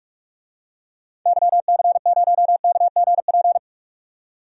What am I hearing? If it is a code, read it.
Morse code, heard as a keyed tone, about 47 wpm, 690 Hz. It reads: YC0KGP
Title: DXCC-100 Call Signs - 6 Characters - 45wpm